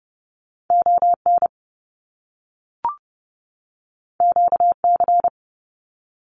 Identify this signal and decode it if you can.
Morse code, heard as a keyed tone, about 30 wpm, 700 Hz. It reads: ON QC
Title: Canadian Provinces and Territories 30wpm